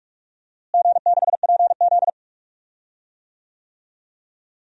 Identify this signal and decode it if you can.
Morse code, heard as a keyed tone, about 42 wpm, 700 Hz. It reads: G6PZ
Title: DXCC-100 Call Signs - 4 Characters - Repeated 3x - 45wpm